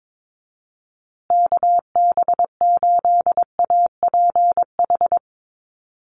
Morse code, heard as a keyed tone, about 22 wpm, 700 Hz. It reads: KB8APH
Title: USA Call Signs Repeated 3x 22wpm